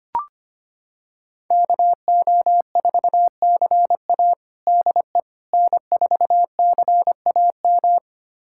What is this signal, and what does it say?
Morse code, heard as a keyed tone, about 25 wpm, 700 Hz. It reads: KO4CA DE N4CAM
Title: USA Callsign DE Callsign - 5 Characters - 25wpm